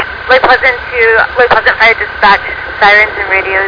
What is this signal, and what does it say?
Emergency Alert System (SAME) tones, heard as harsh buzzing bursts, with speech. It is heard from 0.6 to 3.4 s.